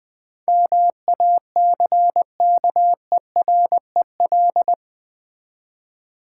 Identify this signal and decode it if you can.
Morse code, heard as a keyed tone, about 21 wpm, 700 Hz. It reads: MACKEREL
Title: Easy Words with Hints - 20wpm